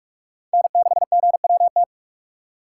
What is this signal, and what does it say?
Morse code, heard as a keyed tone, about 46 wpm, 700 Hz. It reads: N6GWT
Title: USA Call Signs - 5 Characters - 3x Character Spacing - 45wpm